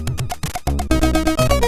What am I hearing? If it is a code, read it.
There is no code.